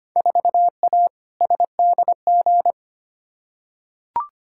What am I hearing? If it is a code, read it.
Morse code, heard as a keyed tone, about 25 wpm, 700 Hz. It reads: 4A SDG
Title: Field Day Exchanges Encore - 25wpm